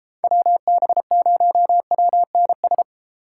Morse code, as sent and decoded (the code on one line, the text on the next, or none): .-- -... ----- .-- -. ...
WB0WNS